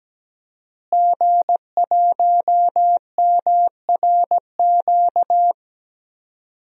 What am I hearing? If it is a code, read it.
Morse code, heard as a keyed tone, about 17 wpm, 700 Hz. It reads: G1MRQ
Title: DXCC-100 Call Signs - 5 Characters - Repeated 2x - 17wpm